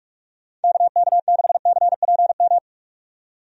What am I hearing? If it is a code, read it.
Morse code, heard as a keyed tone, about 43 wpm, 700 Hz. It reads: KK6CPM
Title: USA Call Signs - 6 Characters - 45wpm